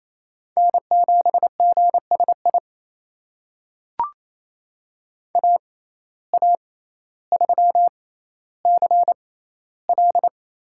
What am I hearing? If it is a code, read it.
Morse code, heard as a keyed tone, about 28 wpm, spaced as for 14 wpm, 700 Hz. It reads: N7GSI AA3CL
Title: USA Call Signs - 5 Characters - 6x Character Spacing - 28wpm